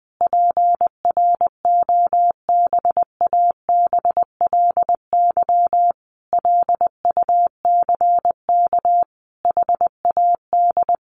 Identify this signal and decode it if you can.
Morse code, heard as a keyed tone, about 19 wpm, 700 Hz. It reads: PROBABLY LUCK HAD
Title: Sentences from Top 1500 Words 20wpm